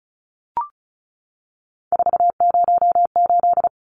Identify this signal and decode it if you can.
Morse code, heard as a keyed tone, about 34 wpm, 700 Hz. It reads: 408
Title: Numbers - 3-Digit 35wpm